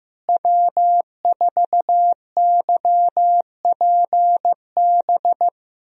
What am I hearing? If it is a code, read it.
Morse code, heard as a keyed tone, about 15 wpm, 700 Hz. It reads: W4YPB
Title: USA Call Signs - 5 Characters - Repeated 2x - 15wpm